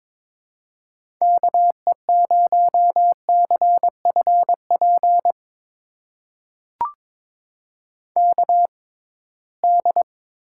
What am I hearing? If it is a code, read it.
Morse code, heard as a keyed tone, about 22 wpm, spaced as for 17 wpm, 700 Hz. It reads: KE0CFP KD
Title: USA Call Signs - 6 Characters - 6x Character Spacing - 22wpm